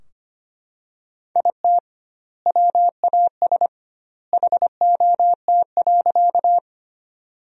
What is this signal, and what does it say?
Morse code, heard as a keyed tone, about 25 wpm, 700 Hz. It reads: IT WAS HOT.
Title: 2x Word-Spacing - Sentences from Top 500 Words - 25wpm